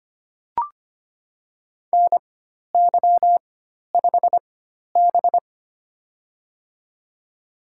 Morse code, heard as a keyed tone, about 25 wpm, spaced as for 12 wpm, 700 Hz. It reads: NY5B
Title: USA Call Signs - 4x Character Spacing - 25wpm